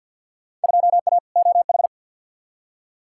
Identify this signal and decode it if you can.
Morse code, heard as a keyed tone, about 50 wpm, 700 Hz. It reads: 2A OH
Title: Field Day Exchanges - 50wpm